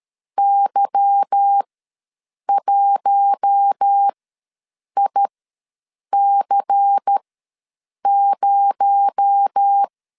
Morse code, heard as a keyed tone, about 13 wpm, spaced as for 7 wpm, 800 Hz. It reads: Y1IC0
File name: LEZIONI / 9 286 ws 12wpm 3351